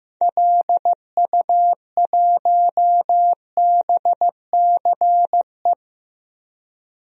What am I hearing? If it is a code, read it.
Morse code, heard as a keyed tone, about 15 wpm, 700 Hz. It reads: LU1BCE
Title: DXCC-100 Call Signs - 6 Characters - 15wpm